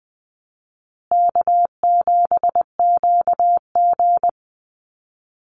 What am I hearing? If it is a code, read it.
Morse code, heard as a keyed tone, about 20 wpm, 700 Hz. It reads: K7QG
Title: USA Call Signs Repeated 3x 20wpm